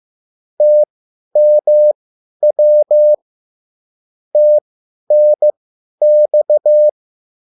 Morse code, heard as a keyed tone, about 15 wpm, spaced as for 10 wpm, 600 Hz. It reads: TMW TNX
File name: QSL / CW lesson